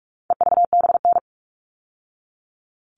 Morse code, heard as a keyed tone, about 44 wpm, 700 Hz. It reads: EVBN
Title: All Cut Numbers - 4-Digit 45wpm